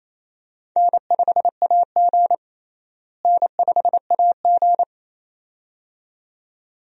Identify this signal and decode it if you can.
Morse code, heard as a keyed tone, about 27 wpm, 700 Hz. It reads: N5AG N5AG
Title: USA Call Signs - 4 Characters - Repeated 2x - 28wpm